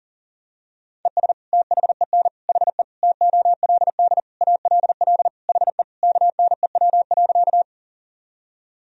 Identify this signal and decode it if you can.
Morse code, heard as a keyed tone, about 37 wpm, 700 Hz. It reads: ES THEN HE TOLD ALL HE KNEW.
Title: Sentences from Top 400 Words and Abbreviations 40wpm